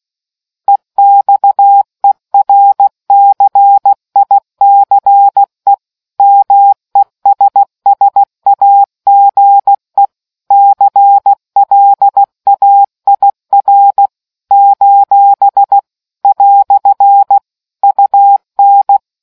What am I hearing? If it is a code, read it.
Morse code, heard as a keyed tone, about 16 wpm, 800 Hz. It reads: EXERCICE MESSAGE CLAIR : " UN